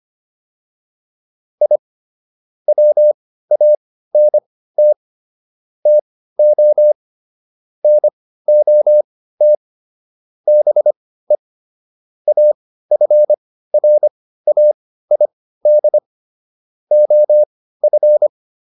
Morse code, heard as a keyed tone, about 25 wpm, spaced as for 15 wpm, 600 Hz. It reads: I WANT TO NOT BE AFRAID OF